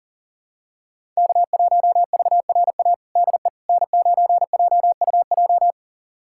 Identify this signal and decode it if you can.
Morse code, heard as a keyed tone, about 40 wpm, 700 Hz. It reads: K1VRA DE N9JUJ
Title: USA Callsign DE Callsign - 5 Characters - 40wpm